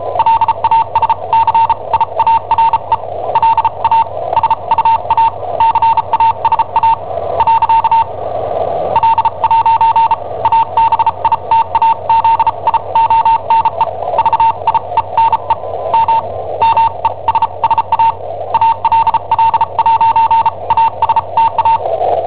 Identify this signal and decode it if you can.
Morse code, heard as a keyed tone, about 34 wpm, spaced as for 29 wpm, 910 Hz, with noise. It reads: LASCIARE LA SUA CASA . L' ABITAZIONE VIENE M MESSA ALL'ASTA
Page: Cassetta esercizi preparata da I7OHP per IW5ELP